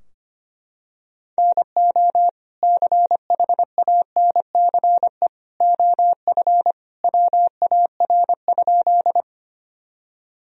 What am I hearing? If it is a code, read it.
Morse code, heard as a keyed tone, about 25 wpm, 700 Hz. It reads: NO CHANCE OF WAR?
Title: Sentences from Top 1000 Words - Review Part 2 - 25wpm